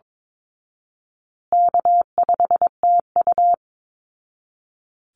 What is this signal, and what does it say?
Morse code, heard as a keyed tone, about 22 wpm, 700 Hz. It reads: K5TU
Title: Call Signs Repeated 3x - 22wpm